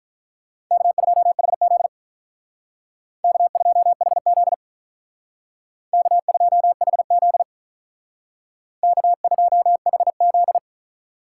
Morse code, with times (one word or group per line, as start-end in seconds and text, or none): 0.7-1.9 K2HZ
3.2-4.5 K2HZ
5.9-7.4 K2HZ
8.8-10.6 K2HZ